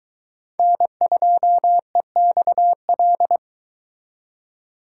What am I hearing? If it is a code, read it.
Morse code, heard as a keyed tone, about 23 wpm, 700 Hz. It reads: N2EXL